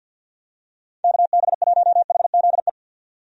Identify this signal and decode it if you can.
Morse code, heard as a keyed tone, about 47 wpm, 700 Hz. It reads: KB1HZE
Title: USA Call Signs - 6 Characters - 50wpm